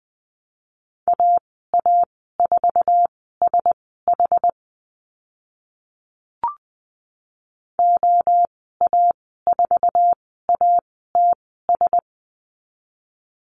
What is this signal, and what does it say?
Morse code, heard as a keyed tone, about 20 wpm, spaced as for 14 wpm, 700 Hz. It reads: AA4SH OA4ATS
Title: 082 Lesson 14g - Number 2 - Call Signs 20wpm